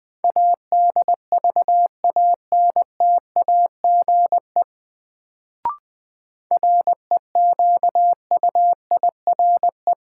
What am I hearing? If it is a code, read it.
Morse code, heard as a keyed tone, about 20 wpm, 700 Hz. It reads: ADVANTAGE REQUIRE
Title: Sentences from Top 900 Words - New Word Drill - 20wpm